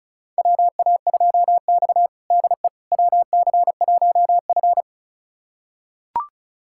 Morse code, heard as a keyed tone, about 36 wpm, 700 Hz. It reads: WA2X DE WC1F
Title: USA Callsign DE Callsign - 4 Characters - 35wpm